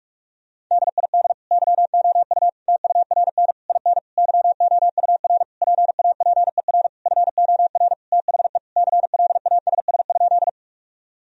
Morse code, heard as a keyed tone, about 46 wpm, 700 Hz. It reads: DID YOU TURN IN YOUR PAPER FOR THE CLASS?
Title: Sentences from Top 1000 Words - Review Part 2 - 45wpm